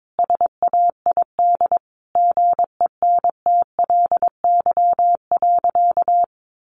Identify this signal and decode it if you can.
Morse code, heard as a keyed tone, about 22 wpm, 700 Hz. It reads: SAID GENTLY.
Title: Sentences from Top 1500 Words 22wpm